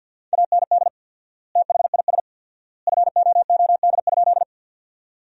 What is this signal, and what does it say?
Morse code, heard as a keyed tone, about 47 wpm, 700 Hz. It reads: AND THIS FOOD?
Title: 4x Word-Spacing - Sentences from Top 300 Words - 50wpm